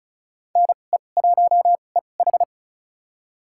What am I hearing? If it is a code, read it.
Morse code, heard as a keyed tone, about 36 wpm, spaced as for 25 wpm, 700 Hz. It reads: NE1EH
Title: USA Call Signs - 5 Characters - 2x Character Spacing - 35wpm